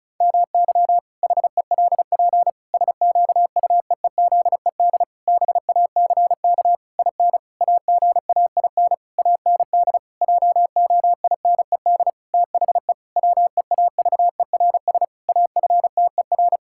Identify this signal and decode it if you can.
Morse code, heard as a keyed tone, about 36 wpm, 700 Hz. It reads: MY HELP SQUEEZED BACK IN AGAIN AND JOINED THE WEAVERS AFTER
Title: Pangram Challenge v2 - 35wpm